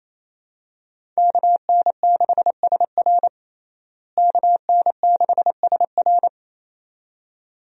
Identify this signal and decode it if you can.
Morse code, heard as a keyed tone, about 28 wpm, 700 Hz. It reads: KN6SR KN6SR
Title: USA Call Signs - 5 Characters - Repeated 2x - 28wpm